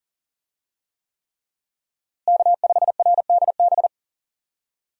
Morse code, heard as a keyed tone, about 40 wpm, 700 Hz. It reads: K5RDB
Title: USA Call Signs - 5 Characters - Repeated 3x - 40wpm